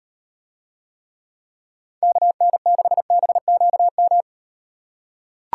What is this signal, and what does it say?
Morse code, heard as a keyed tone, about 38 wpm, 700 Hz. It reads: KN6BQM